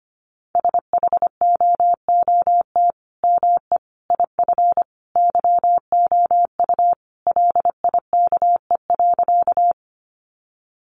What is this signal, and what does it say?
Morse code, heard as a keyed tone, about 25 wpm, 700 Hz. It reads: SHOOT ME IF YOU LIKE.
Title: Sentences from Top 1200 Words 25wpm